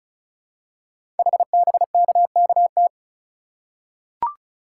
Morse code, heard as a keyed tone, about 36 wpm, 700 Hz. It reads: HBKKT
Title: Code Groups - 5 Letters - 35wpm